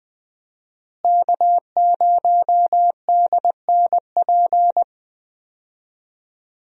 Morse code, heard as a keyed tone, about 21 wpm, 700 Hz. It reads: K0DNP
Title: USA Call Signs - 5 Characters - Repeated 2x - 20wpm